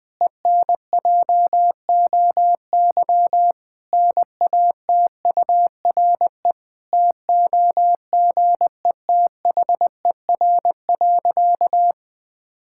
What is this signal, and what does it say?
Morse code, heard as a keyed tone, about 21 wpm, 700 Hz. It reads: ENJOY NATURE TOGETHER.
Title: Difficult Sentences with Hints - 20wpm